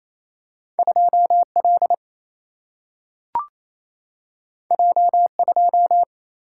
Morse code, heard as a keyed tone, about 28 wpm, 700 Hz. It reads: 2L J2
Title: Single-Letter Single-Number 28wpm